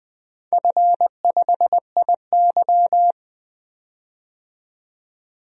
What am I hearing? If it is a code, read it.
Morse code, heard as a keyed tone, about 20 wpm, 700 Hz. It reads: F5IY
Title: DXCC-100 Call Signs - 4 Characters - 20wpm